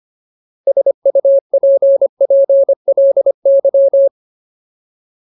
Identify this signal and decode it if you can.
Morse code, heard as a keyed tone, about 25 wpm, 550 Hz. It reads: SUPPLY
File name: Ultimate Word List / 25WPM Ultimate